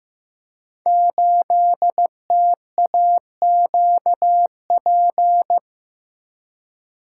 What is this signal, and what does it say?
Morse code, heard as a keyed tone, about 15 wpm, 700 Hz. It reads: 8TAQP
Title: Code Groups - 5 Letter:Numbers - 15wpm